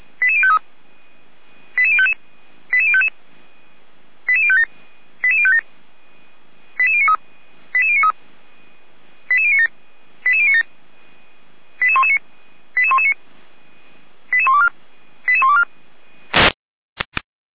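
Five-tone selective call (ZVEI), heard as quick runs of five tones, tones of about 70 ms. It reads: 88053 88055 88055 88057 88057 88093 88093 88097 88097 88119 88119 88124 88124
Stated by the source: SDR-received ZVEI five-tone call 88019, from the audio server of the Orlamünde fire brigade (feuerwehr-orlamuende).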